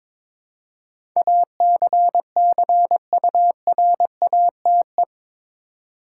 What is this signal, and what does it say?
Morse code, heard as a keyed tone, about 22 wpm, 700 Hz. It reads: ACCURATE
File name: practice / 22wpm / Eight Letter Words - 22wpm